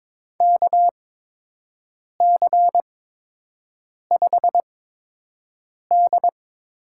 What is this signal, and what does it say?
Morse code, heard as a keyed tone, about 22 wpm, spaced as for 6 wpm, 700 Hz. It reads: KC5D
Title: USA Call Signs - 6 Characters - 8x Character Spacing - 22wpm